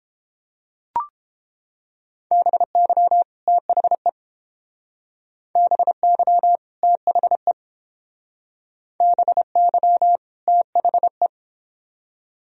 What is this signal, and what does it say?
Morse code, heard as a keyed tone, about 31 wpm, 700 Hz. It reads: BY THE BY THE BY THE